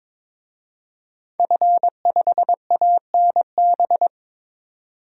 Morse code, heard as a keyed tone, about 22 wpm, 700 Hz. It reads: F5ANB
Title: DXCC-100 Call Signs - 5 Characters - 22wpm